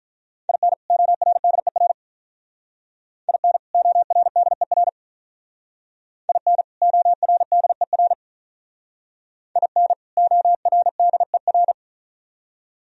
Morse code, heard as a keyed tone, about 45 wpm, 700 Hz. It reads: IN ORDER IN ORDER IN ORDER IN ORDER